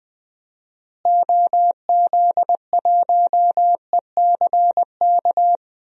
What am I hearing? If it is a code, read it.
Morse code, heard as a keyed tone, about 21 wpm, 700 Hz. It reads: OZ1ECK
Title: DXCC-100 Call Signs - 6 Characters - Repeated 3x - 20wpm